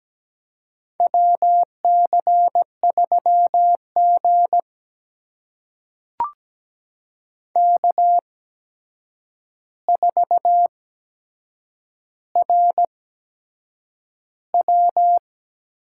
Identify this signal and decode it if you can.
Morse code, heard as a keyed tone, about 17 wpm, spaced as for 7 wpm, 700 Hz. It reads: WC3G K4RW
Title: USA Call Signs - 4 Characters - 8x Character Spacing - 17wpm